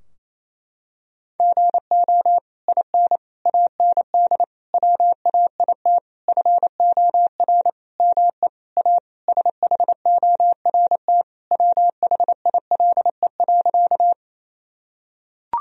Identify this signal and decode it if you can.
Morse code, heard as a keyed tone, about 28 wpm, 700 Hz. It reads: GO IN AND WAIT FOR ME A SHORT WHILE.
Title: Sentences from Top 1000 Words - Review Part 2 - 28wpm